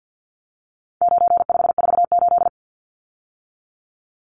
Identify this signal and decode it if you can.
Morse code, heard as a keyed tone, about 50 wpm, 700 Hz. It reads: 9548
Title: Numbers - 4-Digit 50wpm